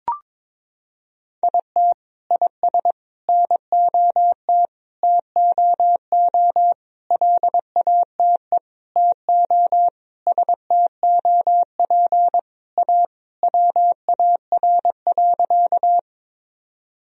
Morse code, heard as a keyed tone, about 21 wpm, 700 Hz. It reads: IT IS NOT TOO LATE TO STOP A WAR.
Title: Sentences from Top 1000 Words - Review Part 2 - 22wpm